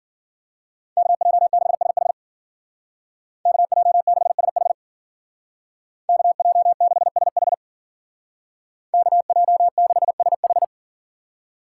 Morse code, heard as a keyed tone, about 49 wpm, 700 Hz. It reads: KJ6SH KJ6SH KJ6SH KJ6SH
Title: Speed Racing - USA 2x2 Call Signs 40wpm